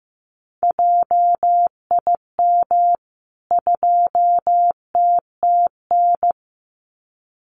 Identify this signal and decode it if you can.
Morse code, heard as a keyed tone, about 15 wpm, 700 Hz. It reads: JIM 2TTN
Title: CWT Exchange 15wpm